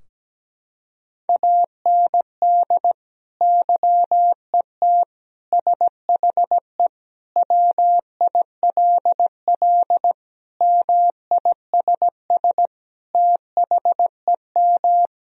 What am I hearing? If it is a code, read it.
Morse code, heard as a keyed tone, about 17 wpm, 700 Hz. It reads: AND YET SHE WILL MISS THEM
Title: Sentences from Top 1000 Words - Review Part 3 - 17wpm